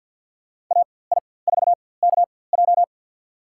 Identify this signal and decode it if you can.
Morse code, heard as a keyed tone, about 46 wpm, spaced as for 23 wpm, 700 Hz. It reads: AI4KJ
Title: USA Call Signs - 5 Characters - 4x Character Spacing - 50wpm